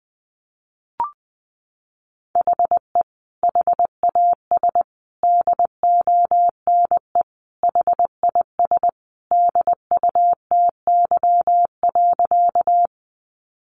Morse code, heard as a keyed tone, about 20 wpm, 700 Hz. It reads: HE HAS DONE HIS DUTY.
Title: Sentences from Top 1500 Words 20wpm